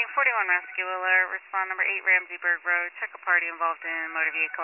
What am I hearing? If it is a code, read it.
Emergency Alert System (SAME) tones, heard as harsh buzzing bursts, with speech. It is heard from 0.3 to 4.3 s.